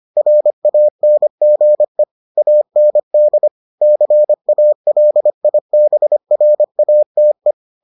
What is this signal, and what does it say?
Morse code, heard as a keyed tone, about 25 wpm, 600 Hz. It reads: RANGE AND CALIBRATE